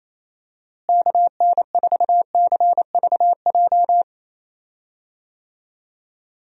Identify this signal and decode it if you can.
Morse code, heard as a keyed tone, about 28 wpm, 700 Hz. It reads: KN4CVJ
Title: USA Call Signs - 6 Characters - Repeated 2x - 28wpm